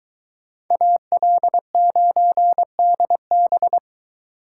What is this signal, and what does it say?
Morse code, heard as a keyed tone, about 23 wpm, 700 Hz. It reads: AL9DB